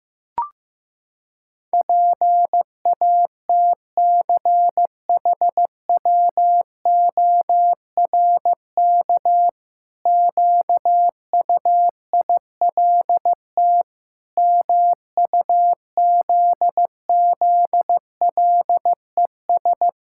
Morse code, heard as a keyed tone, about 15 wpm, 700 Hz. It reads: PATCHWORK QUILT MUZZLES
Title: Pangram Challenge v2 - 15wpm